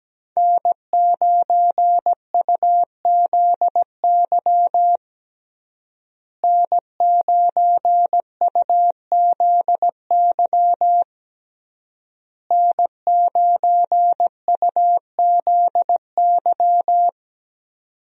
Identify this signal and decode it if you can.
Morse code, heard as a keyed tone, about 17 wpm, 700 Hz. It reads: N9UZY N9UZY N9UZY
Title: USA Call Signs - 5 Characters - Repeated 3x - 17wpm